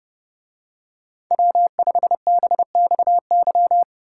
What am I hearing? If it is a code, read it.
Morse code, heard as a keyed tone, about 29 wpm, 700 Hz. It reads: W5BXY